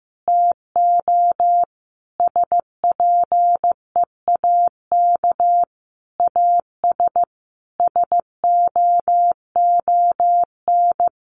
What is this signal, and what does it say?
Morse code, heard as a keyed tone, about 15 wpm, 700 Hz. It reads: TO SPEAK AS SOON
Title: Sentences from Top 1100 Words 15wpm